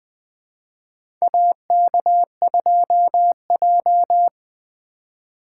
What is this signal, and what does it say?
Morse code, heard as a keyed tone, about 21 wpm, 700 Hz. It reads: AK2J
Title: USA Call Signs - 2x Character Spacing - 20wpm